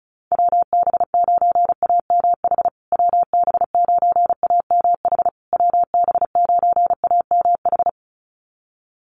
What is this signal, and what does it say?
Morse code, heard as a keyed tone, about 34 wpm, 700 Hz. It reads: WB9AMH WB9AMH WB9AMH
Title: USA Call Signs Repeated 3x 35wpm